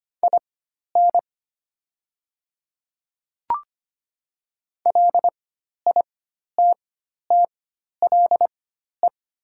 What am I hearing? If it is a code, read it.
Morse code, heard as a keyed tone, about 25 wpm, spaced as for 12 wpm, 700 Hz. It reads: IN LITTLE
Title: 4x Character Spacing - Top 200 Words - 25wpm